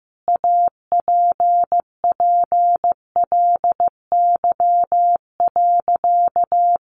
Morse code, heard as a keyed tone, about 15 wpm, 700 Hz. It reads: APPLY.
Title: Sentences from Top 1400 Words 15wpm